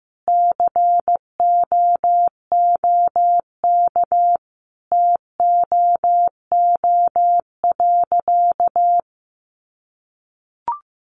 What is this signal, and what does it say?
Morse code, heard as a keyed tone, about 15 wpm, 700 Hz. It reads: COOK TOO.
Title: Sentences from Top 700 Words 15wpm